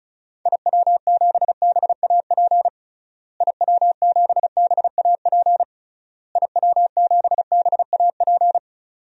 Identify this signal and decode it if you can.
Morse code, heard as a keyed tone, about 36 wpm, 700 Hz. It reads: IW7BAP IW7BAP IW7BAP
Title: DXCC-100 Call Signs - 6 Characters - Repeated 3x - 35wpm